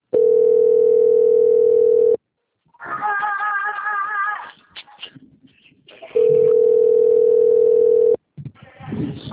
Ringback tone (North America), 440 + 480 Hz.
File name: out- - - - - .21741